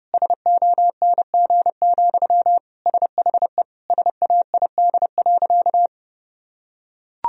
Morse code, heard as a keyed tone, about 30 wpm, 700 Hz. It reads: SONG, SHE SAID.
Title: Sentences from Top 1000 Words - Review Part 2 - 30wpm